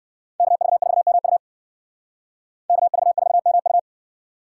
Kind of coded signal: Morse code

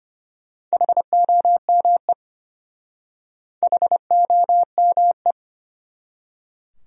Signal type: Morse code

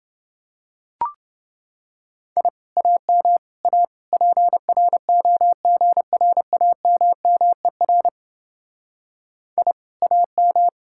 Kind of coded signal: Morse code

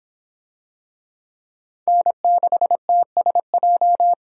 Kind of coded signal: Morse code

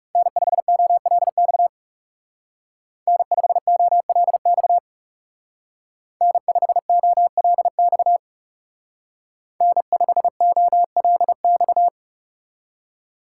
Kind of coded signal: Morse code

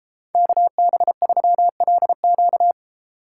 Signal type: Morse code

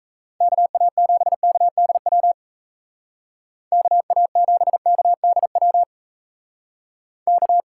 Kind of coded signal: Morse code

